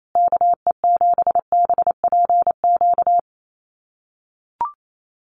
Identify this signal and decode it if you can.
Morse code, heard as a keyed tone, about 28 wpm, 700 Hz. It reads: KE7BPQ
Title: USA Call Signs Repeated 2x 28wpm